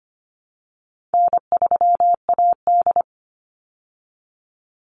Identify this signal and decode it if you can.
Morse code, heard as a keyed tone, about 25 wpm, 700 Hz. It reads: N3AD
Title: Easy Call Signs Repeated 2x - 25wpm